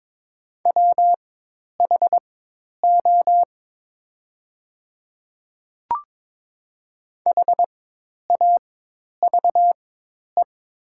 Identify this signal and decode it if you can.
Morse code, heard as a keyed tone, about 21 wpm, spaced as for 10 wpm, 700 Hz. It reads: WHO HAVE
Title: 4x Character Spacing - Top 100 Words - 22wpm